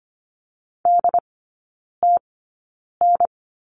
Morse code, heard as a keyed tone, about 25 wpm, spaced as for 9 wpm, 700 Hz. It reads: DTN
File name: course / 25:10wpm / 093 Lesson 16e - Letter c - Three Characters 25wpm